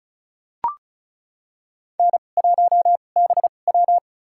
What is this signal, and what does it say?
Morse code, heard as a keyed tone, about 36 wpm, spaced as for 25 wpm, 690 Hz. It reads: N1BW
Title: USA Call Signs - 4 Characters - 2x Character Spacing - 35wpm